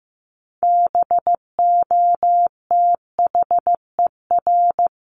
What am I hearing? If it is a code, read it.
Morse code, heard as a keyed tone, about 15 wpm, 700 Hz. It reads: BOTHER
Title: Top 1501-1750 Words 15wpm